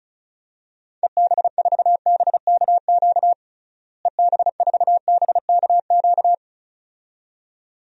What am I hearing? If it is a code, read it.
Morse code, heard as a keyed tone, about 36 wpm, 700 Hz. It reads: EB4BKQ EB4BKQ
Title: DXCC-100 Call Signs - 6 Characters - Repeated 2x - 35wpm